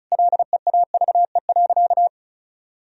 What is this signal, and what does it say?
Morse code, heard as a keyed tone, about 34 wpm, 700 Hz. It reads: LEAVE.
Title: Sentences from Top 1000 Words - Review Part 2 - 35wpm